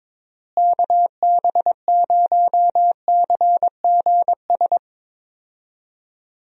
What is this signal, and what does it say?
Morse code, heard as a keyed tone, about 22 wpm, 700 Hz. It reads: KB0CGS
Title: USA Call Signs - 6 Characters - Repeated 2x - 22wpm